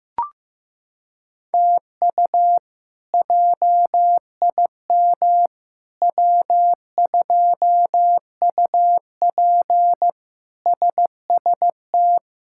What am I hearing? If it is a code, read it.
Morse code, heard as a keyed tone, about 15 wpm, 700 Hz. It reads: TU JIM W2UP SST
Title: K1USN SST QSO Practice - 15wpm